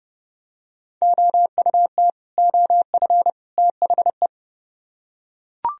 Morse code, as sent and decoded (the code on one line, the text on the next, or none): --- ..- - / --- ..-. / - .... .
OUT OF THE